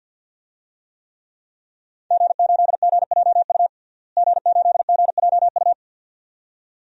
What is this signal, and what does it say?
Morse code, heard as a keyed tone, about 46 wpm, 700 Hz. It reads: G8GJU G8GJU
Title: DXCC-100 Call Signs - 5 Characters - Repeated 2x - 50wpm